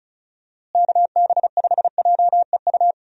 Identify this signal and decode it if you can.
Morse code, heard as a keyed tone, about 36 wpm, 700 Hz. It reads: KB5JEU
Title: USA Call Signs - 6 Characters - Repeated 2x - 35wpm